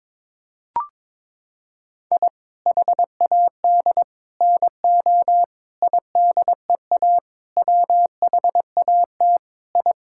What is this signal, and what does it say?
Morse code, heard as a keyed tone, about 22 wpm, 700 Hz. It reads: I HAD NO IDEA WHAT I
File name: practice / 22wpm / Sets of 7 Words - 22wpm